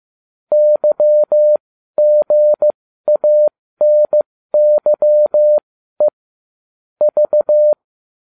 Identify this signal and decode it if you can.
Morse code, heard as a keyed tone, about 15 wpm, spaced as for 12 wpm, 600 Hz. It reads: YGANYE V